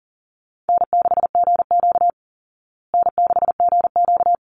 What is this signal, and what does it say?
Morse code, heard as a keyed tone, about 40 wpm, 700 Hz. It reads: N6GQ N6GQ
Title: Easy Call Signs Repeated 2x - 40wpm